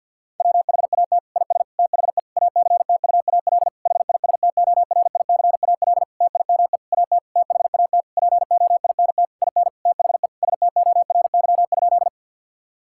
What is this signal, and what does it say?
Morse code, heard as a keyed tone, about 50 wpm, 700 Hz. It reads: WHAT IS THE ACTUAL HISTORICAL TIME AT THAT POINT IN THE STORY?
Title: Sentences from Top 1000 Words - Review Part 5 - 50wpm